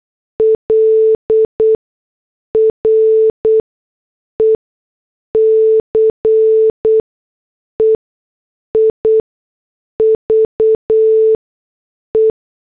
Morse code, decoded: L RECEIVE